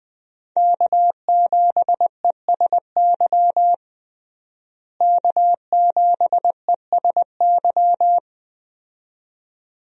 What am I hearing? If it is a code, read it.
Morse code, heard as a keyed tone, about 21 wpm, 700 Hz. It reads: K7ESY K7ESY
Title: USA Call Signs - 5 Characters - Repeated 2x - 20wpm